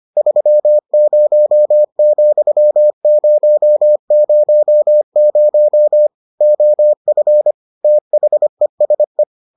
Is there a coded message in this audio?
Yes: Morse code from 0.2 to 9.2 s, heard as a keyed tone, about 25 wpm, 600 Hz. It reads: 30,000 OF THESE